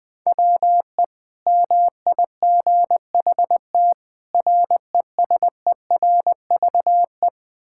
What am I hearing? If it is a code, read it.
Morse code, heard as a keyed tone, about 21 wpm, 700 Hz. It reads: WE MIGHT RESERVE
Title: Sentences from Top 1000 Words - Review Part 6 - 20wpm